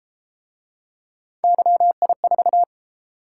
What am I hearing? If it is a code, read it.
Morse code, heard as a keyed tone, about 31 wpm, 700 Hz. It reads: YI4